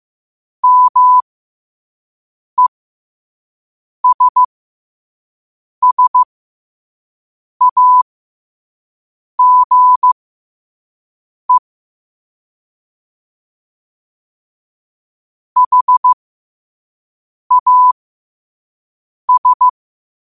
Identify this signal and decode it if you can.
Morse code, heard as a keyed tone, about 15 wpm, spaced as for 5 wpm, 1000 Hz. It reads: MESSAGE HAS
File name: qotdfiles / 5wpm